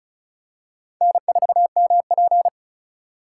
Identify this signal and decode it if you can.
Morse code, heard as a keyed tone, about 34 wpm, 700 Hz. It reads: N4MP
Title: USA Call Signs - 4 Characters - 2x Character Spacing - 35wpm